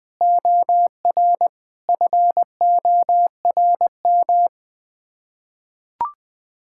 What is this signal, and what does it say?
Morse code, heard as a keyed tone, about 20 wpm, 700 Hz. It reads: OR FORM
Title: Sets of 6 Words - 20wpm